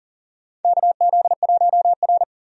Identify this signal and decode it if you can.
Morse code, heard as a keyed tone, about 38 wpm, 700 Hz. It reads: KZ1R
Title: USA Call Signs - 4 Characters - 40wpm